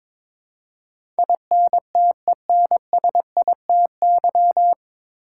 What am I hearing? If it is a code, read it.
Morse code, heard as a keyed tone, about 22 wpm, 700 Hz. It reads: INTENSITY
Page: Morse Code Ninja practice 22wpm